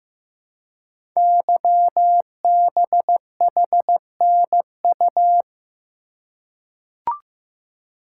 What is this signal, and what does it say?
Morse code, heard as a keyed tone, about 15 wpm, 700 Hz. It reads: YBHNU
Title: Code Groups - 5 Letter:Numbers - 15wpm